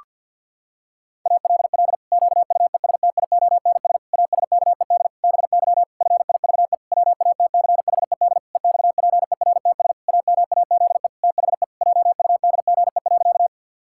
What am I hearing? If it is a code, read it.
Morse code, heard as a keyed tone, about 52 wpm, 700 Hz. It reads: ALL QUESTIONS ASKED BY FIVE WATCHED EXPERTS AMAZE THE JUDGE.
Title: Pangram Challenge v2 - 50wpm